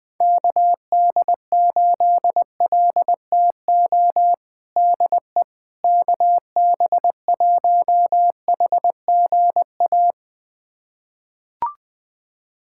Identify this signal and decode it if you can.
Morse code, heard as a keyed tone, about 20 wpm, 700 Hz. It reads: KD8LTO DE KB1HGA
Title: USA Callsign DE Callsign - 6 Characters - 20wpm